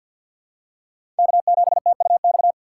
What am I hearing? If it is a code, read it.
Morse code, heard as a keyed tone, about 51 wpm, 700 Hz. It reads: K7TUX
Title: USA Call Signs - 5 Characters - 2x Character Spacing - 50wpm